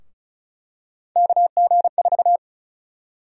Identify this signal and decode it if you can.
Morse code, heard as a keyed tone, about 36 wpm, 700 Hz. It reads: KG4
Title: USA Call Sign Prefixes - 35wpm